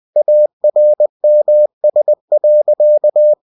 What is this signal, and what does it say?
Morse code, heard as a keyed tone, about 20 wpm, 600 Hz. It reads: ARMS.
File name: cw-books / winnie-the-pooh / winnie-20wpm-page-0098